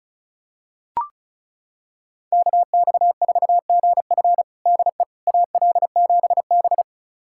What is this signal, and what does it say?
Morse code, heard as a keyed tone, about 34 wpm, 700 Hz. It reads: KX4GF DE AL7B
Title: USA Callsign DE Callsign 35wpm